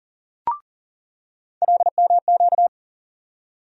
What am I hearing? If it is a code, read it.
Morse code, heard as a keyed tone, about 40 wpm, 700 Hz. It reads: LMQ
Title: USA Call Sign Postfixes - 3 Letters 40wpm